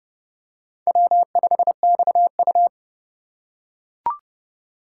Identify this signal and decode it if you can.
Morse code, heard as a keyed tone, about 30 wpm, 700 Hz. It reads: W5XU
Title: K1USN SST QSO Practice - 30wpm